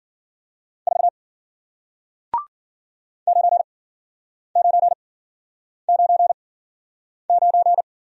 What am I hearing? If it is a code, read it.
Morse code, heard as a keyed tone, about 54 wpm, spaced as for 9 wpm, 700 Hz. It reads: 4 9999